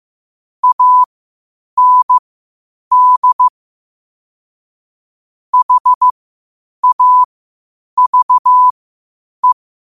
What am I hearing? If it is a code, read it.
Morse code, heard as a keyed tone, about 15 wpm, spaced as for 8 wpm, 1000 Hz. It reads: AND HAVE
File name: qotdfiles / 8wpm